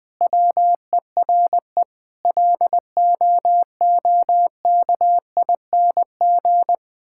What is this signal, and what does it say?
Morse code, heard as a keyed tone, about 21 wpm, 700 Hz. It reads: WERE LOOKING
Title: Sentences from Top 1000 Words - Review Part 6 - 20wpm